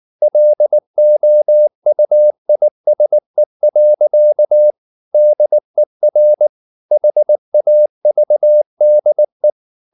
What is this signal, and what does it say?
Morse code, heard as a keyed tone, about 19 wpm, 600 Hz. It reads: LOUISE. DER HAVDE
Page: Ferie 19 wpm